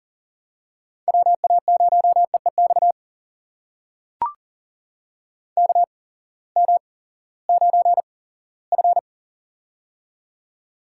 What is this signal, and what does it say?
Morse code, heard as a keyed tone, about 40 wpm, spaced as for 20 wpm, 700 Hz. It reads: WA0EEX KM9F
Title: USA Call Signs - 8x Character Spacing - 40wpm